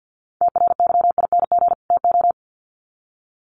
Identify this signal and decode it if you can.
Morse code, heard as a keyed tone, about 50 wpm, 700 Hz. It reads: TRYING TO
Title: Sets of 2 Words 50wpm